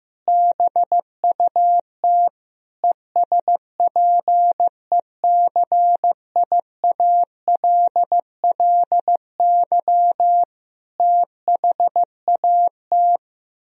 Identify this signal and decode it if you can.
Morse code, heard as a keyed tone, about 15 wpm, 700 Hz. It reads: BUT ESPECIALLY THAT
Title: Sentences from Top 1000 Words - Review Part 4 - 15wpm